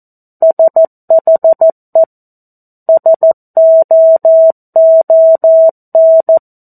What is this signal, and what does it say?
Morse code, heard as a keyed tone, about 14 wpm, 650 Hz. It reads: SHE SOON
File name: 02-Chapter-02 14 WPM